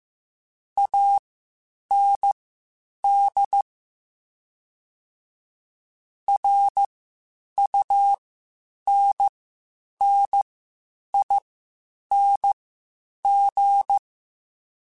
Morse code, heard as a keyed tone, about 15 wpm, spaced as for 8 wpm, 800 Hz. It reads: AND RUNNING
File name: (spaced) / QSOs Spaced-mp3s / WD0FBT 1 @13 sp